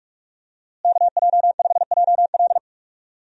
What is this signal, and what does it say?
Morse code, heard as a keyed tone, about 47 wpm, 700 Hz. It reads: KJ5JL
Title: USA Call Signs - 5 Characters - Repeated 3x - 45wpm